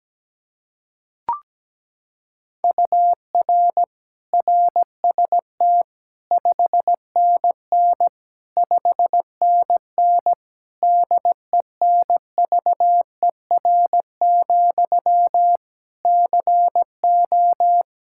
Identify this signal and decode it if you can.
Morse code, heard as a keyed tone, about 17 wpm, 700 Hz. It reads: UR RST 5NN 5NN DENVER, CO
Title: Practice QSO - Exchange 1 - Beginning - 17wpm